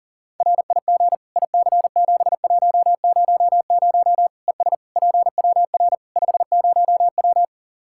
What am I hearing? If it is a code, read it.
Morse code, heard as a keyed tone, about 37 wpm, 700 Hz. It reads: RIG IC7100 ES PWR 50W
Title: Practice QSO - Exchange 2 - Info - 40wpm